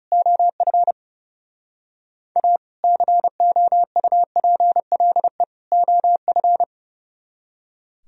Morse code, heard as a keyed tone, about 37 wpm, 700 Hz. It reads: OF A COUPLE OF
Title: Speed-Racing - Sets of 3 Words 30wpm